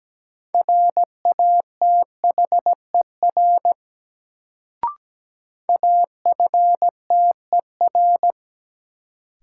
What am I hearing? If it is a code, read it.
Morse code, heard as a keyed tone, about 17 wpm, 700 Hz. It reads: RATHER AFTER
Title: Sets of 2 Words - New Word Drill - 17wpm